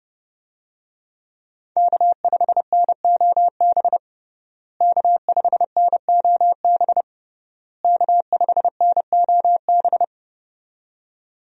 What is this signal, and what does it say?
Morse code, heard as a keyed tone, about 28 wpm, 700 Hz. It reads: K5NOB K5NOB K5NOB
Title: USA Call Signs - 5 Characters - Repeated 3x - 30wpm